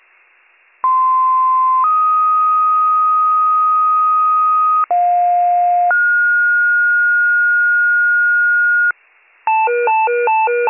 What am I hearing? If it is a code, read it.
Two-tone page, heard as a one-second tone, then a three-second tone, about 1040 then 1250 Hz, about 710 then 1430 Hz, with noise. It is heard from 0.8 to 8.9 s.